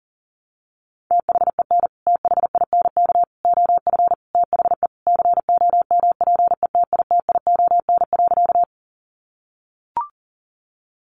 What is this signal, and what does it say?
Morse code, heard as a keyed tone, about 40 wpm, 700 Hz. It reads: THEN THINK OF THE COMPETITION.
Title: Sentences from Top 1000 Words 40wpm